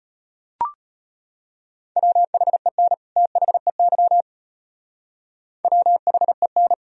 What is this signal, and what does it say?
Morse code, heard as a keyed tone, about 38 wpm, 700 Hz. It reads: WHEN THEY WHEN